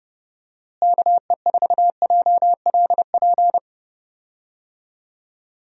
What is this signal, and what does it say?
Morse code, heard as a keyed tone, about 31 wpm, 700 Hz. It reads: KE4JLP
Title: USA Call Signs - 6 Characters - Repeated 3x - 30wpm